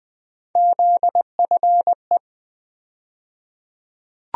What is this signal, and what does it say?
Morse code, heard as a keyed tone, about 20 wpm, 700 Hz. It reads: ZFE
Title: USA Call Sign Postfixes - 3 Letters 20wpm